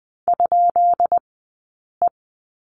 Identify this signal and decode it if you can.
Morse code, heard as a keyed tone, about 20 wpm, 700 Hz. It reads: ? E
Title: 132 Lesson 22d - Question Mark - Two Characters 20wpm